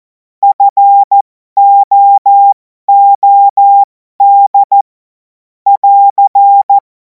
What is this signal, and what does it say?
Morse code, heard as a keyed tone, about 14 wpm, spaced as for 12 wpm, 800 Hz. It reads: FOOD <AR>
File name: Morsekurs / Witze